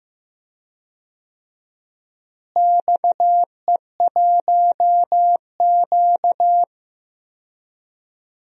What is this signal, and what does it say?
Morse code, heard as a keyed tone, about 15 wpm, 700 Hz. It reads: XE1Q
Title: DXCC-100 Call Signs - 4 Characters - Repeated 2x - 15wpm